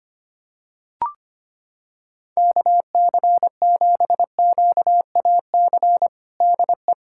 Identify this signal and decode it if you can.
Morse code, heard as a keyed tone, about 25 wpm, 700 Hz. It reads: KC7QAC DE
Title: USA Callsign DE Callsign - 6 Characters - 25wpm